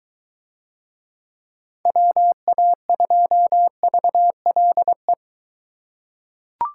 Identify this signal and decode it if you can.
Morse code, heard as a keyed tone, about 23 wpm, 700 Hz. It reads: WA2VLE